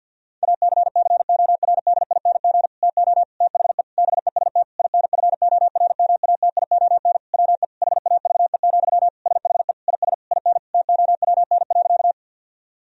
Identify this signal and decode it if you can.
Morse code, heard as a keyed tone, about 50 wpm, 700 Hz. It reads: ACCORDING TO THE BEST INFORMATION WE HAVE, SHE IS IN TOWN.
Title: Sentences from Top 1000 Words - Review Part 4 - 50wpm